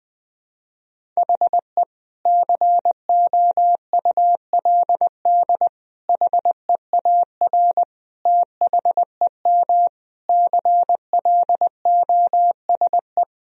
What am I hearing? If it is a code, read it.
Morse code, heard as a keyed tone, about 21 wpm, 700 Hz. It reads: HE COULD HEAR THEM CLOSE
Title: Sentences from Top 1000 Words - Review Part 1 - 20wpm